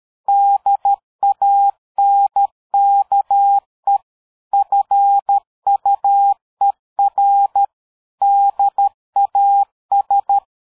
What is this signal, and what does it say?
Morse code, heard as a keyed tone, about 13 wpm, 800 Hz. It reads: DANKE FUER DAS
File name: dk5ke / PersWorte 60